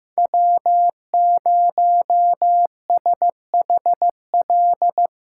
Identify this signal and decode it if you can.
Morse code, heard as a keyed tone, about 15 wpm, 700 Hz. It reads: W0SHL
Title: K1USN SST QSO Practice - 15wpm